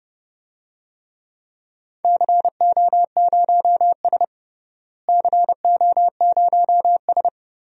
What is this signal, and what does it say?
Morse code, heard as a keyed tone, about 30 wpm, 700 Hz. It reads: CO0S CO0S
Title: DXCC-100 Call Signs - 4 Characters - Repeated 3x - 30wpm